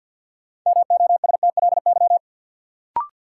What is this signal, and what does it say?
Morse code, heard as a keyed tone, about 53 wpm, 700 Hz. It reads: MOSTLY
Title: Sentences from Top 1300 Words - New Word Drill Alt - 50wpm